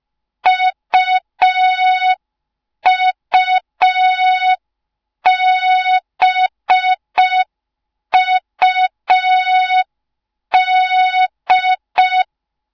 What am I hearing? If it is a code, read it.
Morse code, heard as a keyed tone, about 5 wpm, 3700 Hz. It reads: UUBUD